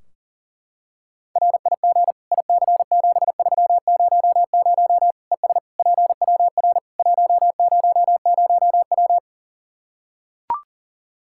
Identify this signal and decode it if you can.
Morse code, heard as a keyed tone, about 40 wpm, 700 Hz. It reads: RIG IC7300 ES PWR 100W
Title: Practice QSO - Exchange 2 - Info - 40wpm